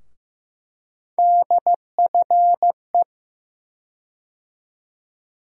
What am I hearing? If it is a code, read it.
Morse code, heard as a keyed tone, about 15 wpm, 700 Hz. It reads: DFE
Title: USA Call Sign Postfixes - 3 Letters 15wpm